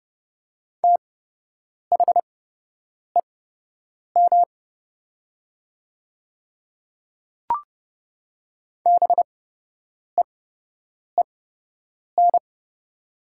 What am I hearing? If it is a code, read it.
Morse code, heard as a keyed tone, about 29 wpm, spaced as for 8 wpm, 700 Hz. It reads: THEM BEEN
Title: 8x Character Spacing - Top 100 Words - 30wpm